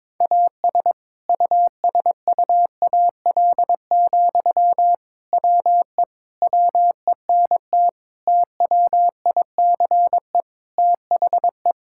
Morse code, heard as a keyed tone, about 22 wpm, 700 Hz. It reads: AS USUAL, WE WENT TWICE THE
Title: Sentences from Top 1600 Words 22wpm